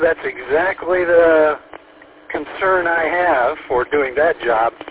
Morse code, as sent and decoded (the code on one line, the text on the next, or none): none